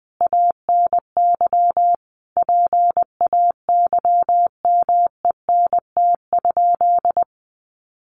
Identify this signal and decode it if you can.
Morse code, heard as a keyed tone, about 20 wpm, 700 Hz. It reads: ANY PAYMENT?
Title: Sentences from Top 1200 Words 20wpm